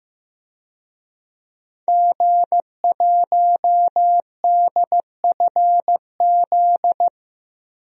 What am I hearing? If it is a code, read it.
Morse code, heard as a keyed tone, about 15 wpm, 700 Hz. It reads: G1DFZ
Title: DXCC-100 Call Signs - 5 Characters - Repeated 3x - 15wpm